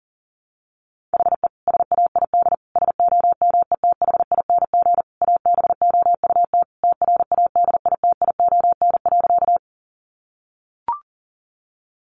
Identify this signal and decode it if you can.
Morse code, heard as a keyed tone, about 40 wpm, 700 Hz. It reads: HE SAID SOMETHING ABOUT TRADITION.
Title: Sentences from Top 1500 Words 40wpm